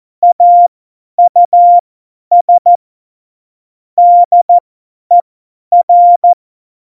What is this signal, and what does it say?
Morse code, heard as a keyed tone, about 14 wpm, spaced as for 10 wpm, 700 Hz. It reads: AUS DER
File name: Morsekurs / Witze / Witz 16. April 2019, 50 F70